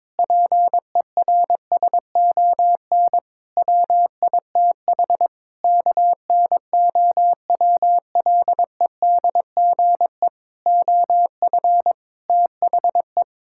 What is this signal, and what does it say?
Morse code, heard as a keyed tone, about 22 wpm, 700 Hz. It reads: PERSON WITH KNOWLEDGE OF THE
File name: Sets of 9 Words - 22wpm